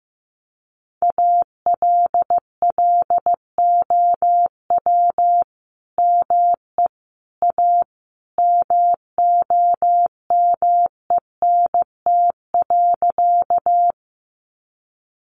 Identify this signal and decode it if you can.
Morse code, heard as a keyed tone, about 15 wpm, 700 Hz. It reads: ALLOW ME A MOMENT.
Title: Sentences from Top 1000 Words 15wpm